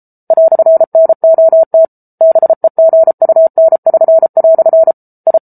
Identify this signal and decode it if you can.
Morse code, heard as a keyed tone, about 33 wpm, 650 Hz. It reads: "NOT BEGUN<SN>" I